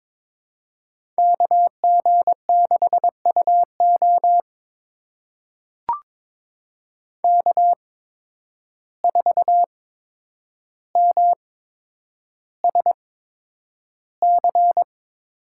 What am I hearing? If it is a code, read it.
Morse code, heard as a keyed tone, about 22 wpm, spaced as for 9 wpm, 700 Hz. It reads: KG6UO K4MSC
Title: USA Call Signs - 5 Characters - 8x Character Spacing - 22wpm